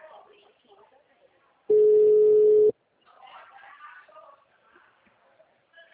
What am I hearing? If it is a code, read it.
Ringback tone (Europe, 425 Hz), with speech.